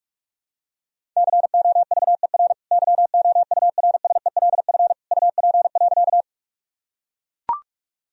Morse code, heard as a keyed tone, about 41 wpm, 700 Hz. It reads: COVER YOURSELF UP.
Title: Sentences from Top 1000 Words - Review Part 1 - 45wpm